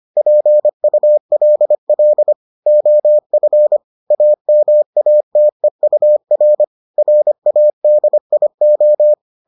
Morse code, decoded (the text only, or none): PULL OF AMATEUR RADIO